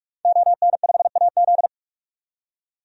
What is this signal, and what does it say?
Morse code, heard as a keyed tone, about 46 wpm, 700 Hz. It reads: ON5AZ
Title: DXCC-100 Call Signs - 5 Characters - 45wpm